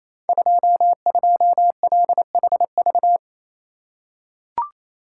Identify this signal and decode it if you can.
Morse code, heard as a keyed tone, about 28 wpm, 700 Hz. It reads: 22LHV
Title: Code Groups - 5 Letter:Numbers - 28wpm